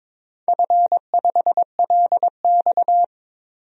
Morse code, heard as a keyed tone, about 22 wpm, 700 Hz. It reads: F5LX
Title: DXCC-100 Call Signs - 4 Characters - 22wpm